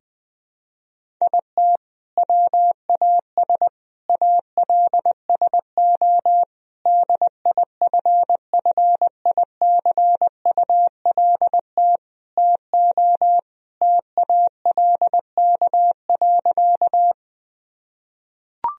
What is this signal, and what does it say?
Morse code, heard as a keyed tone, about 20 wpm, 700 Hz. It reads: IT WAS ALSO DIFFICULT TO TALK.
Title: Sentences from Top 1000 Words - Review Part 1 - 20wpm